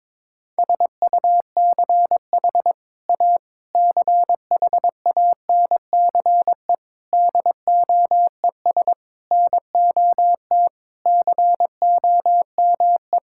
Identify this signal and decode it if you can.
Morse code, heard as a keyed tone, about 22 wpm, 700 Hz. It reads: SUCH A CHANCE DOES NOT COME
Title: Sentences from Top 1600 Words 22wpm